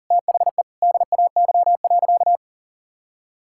Morse code, heard as a keyed tone, about 40 wpm, 700 Hz. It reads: THE DAY.
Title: Sentences from Top 1600 Words 40wpm